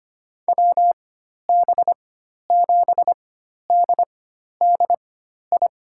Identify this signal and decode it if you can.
Morse code, heard as a keyed tone, about 25 wpm, spaced as for 12 wpm, 700 Hz. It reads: WB7DDI
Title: USA Call Signs - 6 Characters - 4x Character Spacing - 25wpm